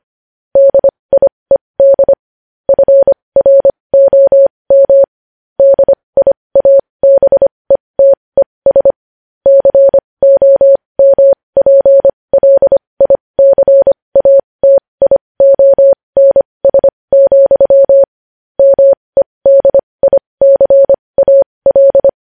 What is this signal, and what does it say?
Morse code, heard as a keyed tone, about 25 wpm, spaced as for 20 wpm, 550 Hz. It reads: DIED FROM DIABETES COMPLICATIONS, MEDICAL
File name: podcast / News headlines-1x-25x20